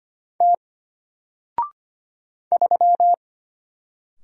Morse code, heard as a keyed tone, about 25 wpm, 700 Hz. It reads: T 3
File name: Single-Letter-Number-25wpm